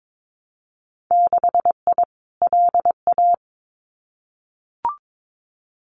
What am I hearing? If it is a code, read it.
Morse code, heard as a keyed tone, about 22 wpm, 700 Hz. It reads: 6I LA
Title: Winter Field Day Exchanges Encore 22wpm